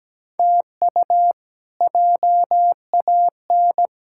Morse code, heard as a keyed tone, about 17 wpm, 700 Hz. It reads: TU JAN
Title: K1USN SST QSO Practice - 17wpm